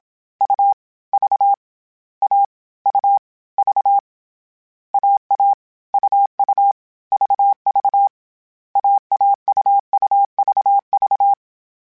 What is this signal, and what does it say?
Morse code, heard as a keyed tone, about 27 wpm, 800 Hz. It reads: U V A U V AA UU VV AAUUVV